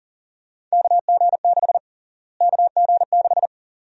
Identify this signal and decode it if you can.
Morse code, heard as a keyed tone, about 40 wpm, 700 Hz. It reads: KG6 KG6
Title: USA Call Sign Prefixes - Repeated 2x - 40wpm